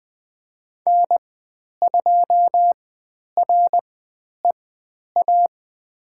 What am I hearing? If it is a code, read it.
Morse code, heard as a keyed tone, about 20 wpm, spaced as for 10 wpm, 700 Hz. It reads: N2REA
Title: 102 Lesson 17g - Letter u - Call Signs 20wpm